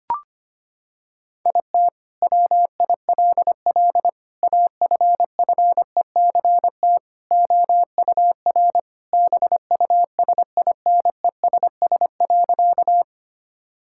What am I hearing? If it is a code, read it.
Morse code, heard as a keyed tone, about 25 wpm, 700 Hz. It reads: IT WILL AFFECT OUR BUSINESS.
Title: Sentences from Top 1000 Words - Review Part 6 - 25wpm